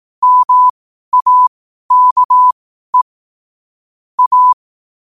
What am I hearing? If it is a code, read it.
Morse code, heard as a keyed tone, about 18 wpm, spaced as for 12 wpm, 1000 Hz. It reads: MAKE A
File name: qotdfiles / 12wpm